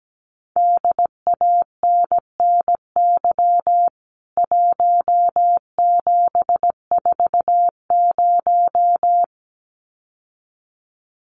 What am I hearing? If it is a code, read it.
Morse code, heard as a keyed tone, about 17 wpm, 700 Hz. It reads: DANNY 1740
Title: CWT Exchange 17wpm